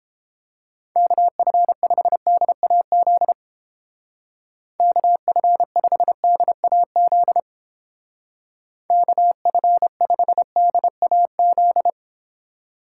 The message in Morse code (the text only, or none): KF5DAZ KF5DAZ KF5DAZ